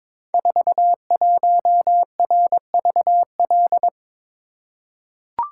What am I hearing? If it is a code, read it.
Morse code, heard as a keyed tone, about 23 wpm, 700 Hz. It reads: 41RVL
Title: Code Groups - 5 Letter:Numbers - 22wpm